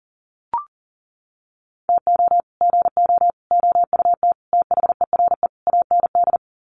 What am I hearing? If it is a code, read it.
Morse code, heard as a keyed tone, about 40 wpm, 700 Hz. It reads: TO GO OUT THERE AND
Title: Sets of 5 Words Encore2 40wpm